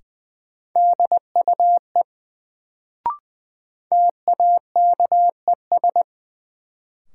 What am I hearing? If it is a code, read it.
Morse code, heard as a keyed tone, about 20 wpm, 700 Hz. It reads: DUE TAKES
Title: Sets of 5 Words Encore - New Word Drill - 20wpm